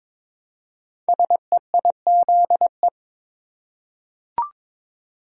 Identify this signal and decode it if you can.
Morse code, heard as a keyed tone, about 22 wpm, 700 Hz. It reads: SEIZE
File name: 5 Letter Words - First 3 Letters Given - 22wpm